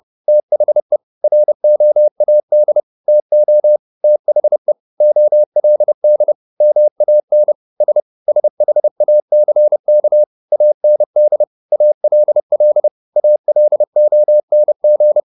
Morse code, heard as a keyed tone, about 30 wpm, 600 Hz. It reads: THE ROAD TO THE OLD MAN S SHACK AND ALL ALONG
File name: the old man and the sea / the old man and the sea 1 30wpm